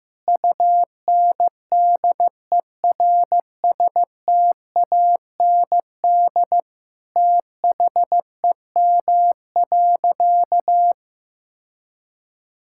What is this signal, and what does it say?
Morse code, heard as a keyed tone, about 15 wpm, 700 Hz. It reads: UNDERSTAND THEM.
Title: Sentences from Top 1000 Words - Review Part 5 - 15wpm